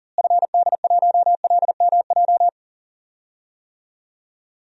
Morse code, heard as a keyed tone, about 40 wpm, 700 Hz. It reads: FD1LMJ
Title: DXCC-100 Call Signs - 6 Characters - Repeated 2x - 40wpm